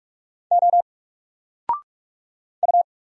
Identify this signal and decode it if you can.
Morse code, heard as a keyed tone, about 46 wpm, 700 Hz. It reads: O U
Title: Single Letter-Number Alt 45wpm